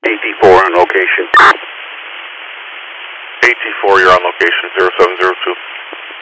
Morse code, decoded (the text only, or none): none